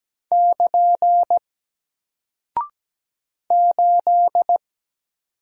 Morse code, heard as a keyed tone, about 17 wpm, 700 Hz. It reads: <KN> 8
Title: Single Character (Letter:Number:Punctuation:Prosign) 17wpm